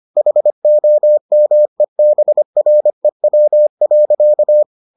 Morse code, heard as a keyed tone, about 25 wpm, 600 Hz. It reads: HOMEBREW.